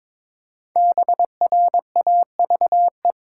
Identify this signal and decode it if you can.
Morse code, heard as a keyed tone, about 22 wpm, 700 Hz. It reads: BRAVE
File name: 5 Letter Words - First 3 Letters Given - 22wpm